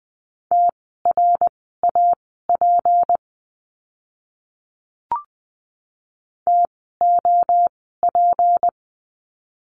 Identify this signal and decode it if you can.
Morse code, heard as a keyed tone, about 20 wpm, spaced as for 15 wpm, 700 Hz. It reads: TRAP TOP
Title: 153 Lesson 25f - Letter p - Words 20wpm